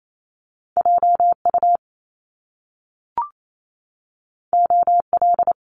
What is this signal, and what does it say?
Morse code, heard as a keyed tone, about 28 wpm, 700 Hz. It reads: JU OL
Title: Top Two Letter Combinations in Words 28wpm